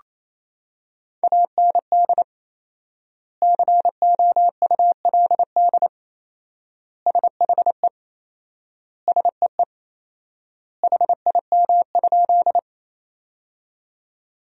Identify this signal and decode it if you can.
Morse code, heard as a keyed tone, about 27 wpm, 700 Hz. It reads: AND COULD SHE SEE HIM?
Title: 4x Word-Spacing - Sentences from Top 100 Words - 28wpm